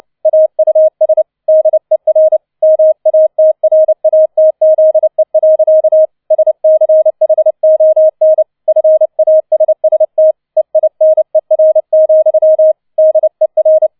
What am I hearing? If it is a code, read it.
Morse code, heard as a keyed tone, about 29 wpm, 630 Hz. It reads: AUS DER MATRATZE. SCHON FASST EINER, DER